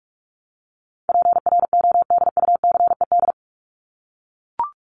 Morse code, heard as a keyed tone, about 44 wpm, 700 Hz. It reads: PRODUCED
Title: ED Suffix - Top 100 Words with Base Words 45wpm